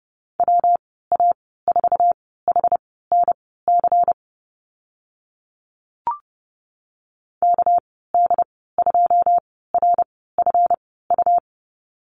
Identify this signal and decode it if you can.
Morse code, heard as a keyed tone, about 30 wpm, spaced as for 17 wpm, 700 Hz. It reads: WA4HNC KD2RFU
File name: Lesson 37g - Letter x - Call Signs 30wpm